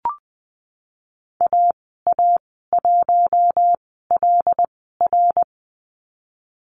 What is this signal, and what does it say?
Morse code, heard as a keyed tone, about 20 wpm, spaced as for 15 wpm, 700 Hz. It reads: AA1LR
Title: 076 Lesson 13g - Letter L - Call Signs 20wpm